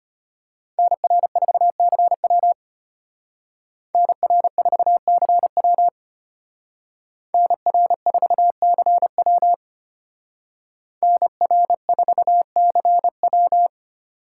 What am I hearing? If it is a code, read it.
Morse code, heard as a keyed tone, about 32 wpm, 700 Hz. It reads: NR4CW NR4CW NR4CW NR4CW